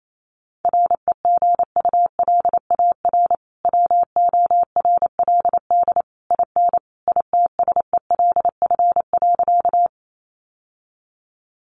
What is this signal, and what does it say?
Morse code, heard as a keyed tone, about 28 wpm, 700 Hz. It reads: REGULAR WORLD IN ITSELF.
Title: Sentences from Top 900 Words 28wpm